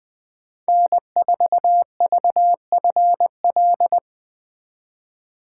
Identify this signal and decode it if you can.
Morse code, heard as a keyed tone, about 21 wpm, 700 Hz. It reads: N4VFL
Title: USA Call Signs - 2x Character Spacing - 20wpm